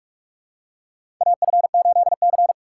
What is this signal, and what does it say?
Morse code, heard as a keyed tone, about 46 wpm, 700 Hz. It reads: AF8C
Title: USA Call Signs - 4 Characters - 6x Character Spacing - 45wpm